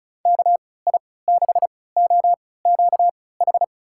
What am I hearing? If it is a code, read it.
Morse code, heard as a keyed tone, about 36 wpm, spaced as for 20 wpm, 700 Hz. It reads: KI6OQH
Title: USA Call Signs - 6 Characters - 3x Character Spacing - 35wpm